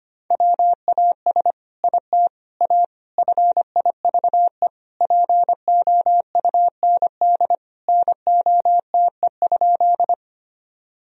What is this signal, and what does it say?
Morse code, heard as a keyed tone, about 25 wpm, 700 Hz. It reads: WAS IT A FIVE POUND NOTE?
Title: Sentences from Top 1000 Words - Review Part 2 - 25wpm